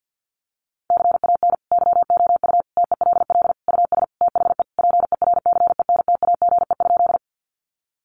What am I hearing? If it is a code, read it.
Morse code, heard as a keyed tone, about 49 wpm, 700 Hz. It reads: CAN YOU TELL US THE PERCENTAGE?
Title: Sentences from Top 1300 Words 50wpm